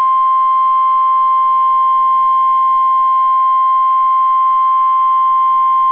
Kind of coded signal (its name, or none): Emergency Alert System (SAME) tones